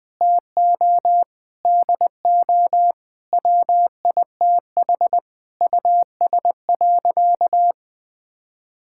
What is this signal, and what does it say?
Morse code, heard as a keyed tone, about 20 wpm, 700 Hz. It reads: TO DO WITH US.
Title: Sentences with 8-Word Phrases - 20wpm